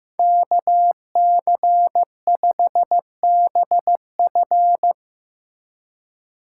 Morse code, heard as a keyed tone, about 17 wpm, 700 Hz. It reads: KC5BF
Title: Speed Racing - USA 2x2 Call Signs 15wpm